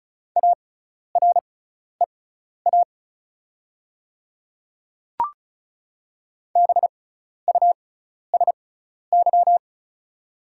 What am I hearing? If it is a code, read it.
Morse code, heard as a keyed tone, about 36 wpm, spaced as for 12 wpm, 700 Hz. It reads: AREA BUSY
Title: 6x Character Spacing - Top 500 Words - 35wpm